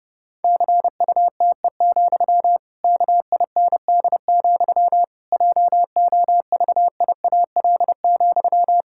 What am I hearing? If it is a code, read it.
Morse code, heard as a keyed tone, about 30 wpm, 700 Hz. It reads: CUTE, KIND, JOVIAL,
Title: Pangram Challenge v2 - 30wpm